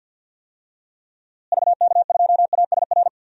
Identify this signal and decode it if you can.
Morse code, heard as a keyed tone, about 52 wpm, 700 Hz. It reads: VK2ASR
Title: DXCC-100 Call Signs - 6 Characters - Repeated 2x - 50wpm